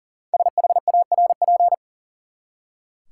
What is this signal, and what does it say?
Morse code, heard as a keyed tone, about 40 wpm, 700 Hz. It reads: SHARP
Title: Five Letter Words - 40wpm